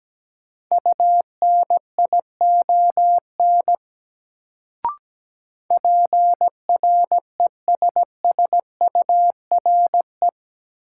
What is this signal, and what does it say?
Morse code, heard as a keyed tone, about 17 wpm, 700 Hz. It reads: UNION PRESSURE
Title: Sentences from Top 1400 Words - New Word Drill Alt - 17wpm